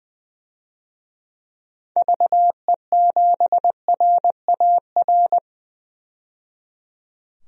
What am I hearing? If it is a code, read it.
Morse code, heard as a keyed tone, about 20 wpm, 700 Hz. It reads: VE7RAR
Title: DXCC-100 Call Signs - 6 Characters - 20wpm